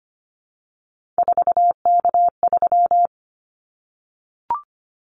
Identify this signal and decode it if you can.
Morse code, heard as a keyed tone, about 25 wpm, 700 Hz. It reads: 4K3
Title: Callsign Prefixes for Top 100 DXCC - 25wpm